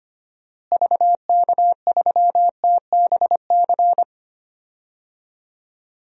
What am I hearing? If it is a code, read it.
Morse code, heard as a keyed tone, about 25 wpm, 700 Hz. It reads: VK3TBC
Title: DXCC-100 Call Signs - 6 Characters - Repeated 3x - 25wpm